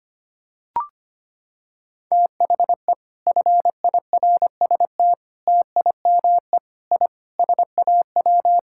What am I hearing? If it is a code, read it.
Morse code, heard as a keyed tone, about 25 wpm, 700 Hz. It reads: THE FIRST TIME I SAW
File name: practice / 25wpm / Sentences with 4-Word Phrases - 25wpm